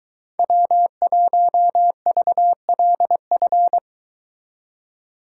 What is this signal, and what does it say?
Morse code, heard as a keyed tone, about 23 wpm, 700 Hz. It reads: W1VLF